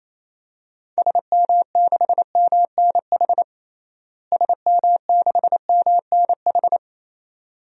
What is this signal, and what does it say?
Morse code, heard as a keyed tone, about 28 wpm, 700 Hz. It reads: SM6MNH SM6MNH
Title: DXCC-100 Call Signs - 6 Characters - Repeated 2x - 28wpm